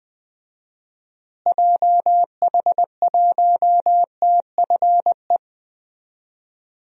Morse code, heard as a keyed tone, about 20 wpm, 700 Hz. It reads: JH1TFE
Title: DXCC-100 Call Signs - 6 Characters - Repeated 2x - 20wpm